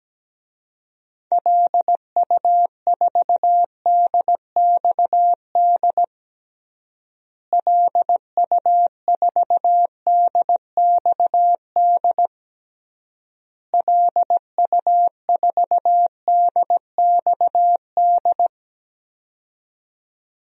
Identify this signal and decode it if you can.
Morse code, heard as a keyed tone, about 17 wpm, 700 Hz. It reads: LU4DXD LU4DXD LU4DXD
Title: DXCC-100 Call Signs - 6 Characters - Repeated 3x - 17wpm